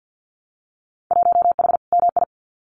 Morse code, heard as a keyed tone, about 50 wpm, 700 Hz. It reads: 1H MI
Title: Winter Field Day Exchanges 50wpm